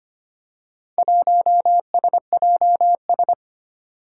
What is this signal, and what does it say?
Morse code, heard as a keyed tone, about 25 wpm, 700 Hz. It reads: 1SJS
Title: Four Challenging Letter-Number Combinations 25wpm